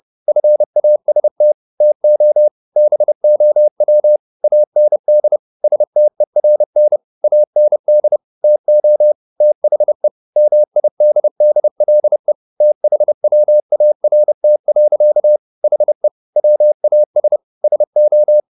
Morse code, heard as a keyed tone, about 30 wpm, 600 Hz. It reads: FAST TO BOW AND STERN AND TO THE MIDDLE THWART. HE WAS SO